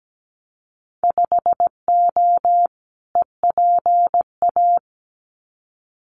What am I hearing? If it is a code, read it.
Morse code, heard as a keyed tone, about 17 wpm, 700 Hz. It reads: 5O EPA
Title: Winter Field Day Exchanges Encore 17wpm